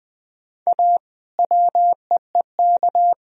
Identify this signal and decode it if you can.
Morse code, heard as a keyed tone, about 20 wpm, 700 Hz. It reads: A WEEK
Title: Sentences from Top 1000 Words - Review Part 4 - 20wpm